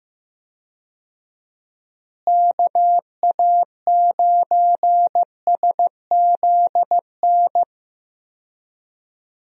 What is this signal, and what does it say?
Morse code, heard as a keyed tone, about 15 wpm, 700 Hz. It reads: KA9SZN
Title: USA Call Signs - 6 Characters - Repeated 2x - 15wpm